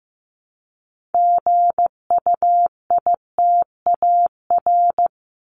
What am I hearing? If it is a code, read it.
Morse code, heard as a keyed tone, about 15 wpm, 700 Hz. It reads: GUITAR
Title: Top 1501-1750 Words 15wpm